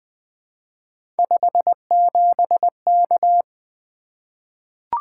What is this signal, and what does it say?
Morse code, heard as a keyed tone, about 20 wpm, 700 Hz. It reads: 57K
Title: Three Letter-Number Combinations - 20wpm